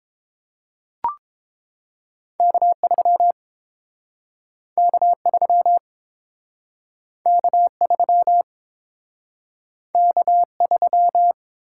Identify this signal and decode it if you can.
Morse code, heard as a keyed tone, about 32 wpm, 700 Hz. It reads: K3 K3 K3 K3